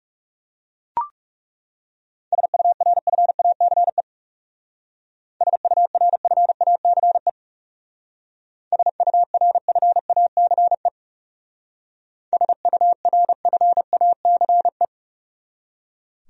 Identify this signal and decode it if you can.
Morse code, heard as a keyed tone, about 36 wpm, 700 Hz. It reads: SURFACE SURFACE SURFACE SURFACE
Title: Speed-Racing Top 500 Words 30wpm